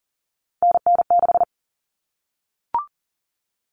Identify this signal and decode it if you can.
Morse code, heard as a keyed tone, about 40 wpm, 700 Hz. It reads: NN6
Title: Common Cut Numbers - 3-Digit 40wpm